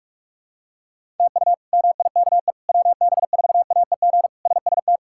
Morse code, heard as a keyed tone, about 46 wpm, 700 Hz. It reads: TU MIKE WB4AEG SST
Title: K1USN SST QSO Practice - 45wpm